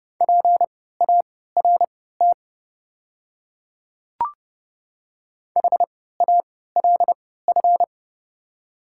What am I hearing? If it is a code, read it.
Morse code, heard as a keyed tone, about 30 wpm, spaced as for 17 wpm, 700 Hz. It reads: PART HALF
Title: 3x Character Spacing - Top 500 Words - 30wpm